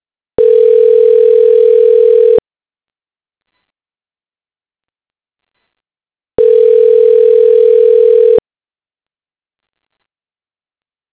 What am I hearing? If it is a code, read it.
Ringback tone (North America), 440 + 480 Hz.
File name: recordings / RE95c1fa13e2e915b4d3783fa78a068db7